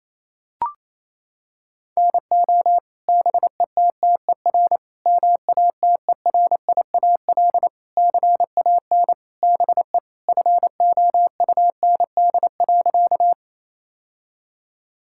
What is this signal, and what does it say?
Morse code, heard as a keyed tone, about 28 wpm, 700 Hz. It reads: NO BETTER MATERIAL CAN BE FOUND.
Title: Sentences from Top 1000 Words - Review Part 5 - 28wpm